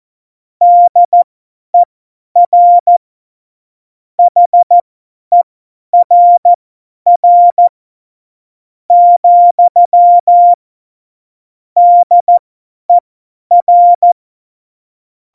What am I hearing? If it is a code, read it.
Morse code, heard as a keyed tone, about 14 wpm, spaced as for 10 wpm, 700 Hz. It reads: DER HERR , DER
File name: Morsekurs / Witze / Witz 13 March 2018, 50 F70